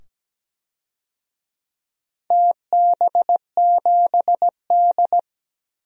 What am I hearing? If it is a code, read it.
Morse code, heard as a keyed tone, about 17 wpm, 700 Hz. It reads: TB7D
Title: DXCC-100 Call Signs - 4 Characters - 17wpm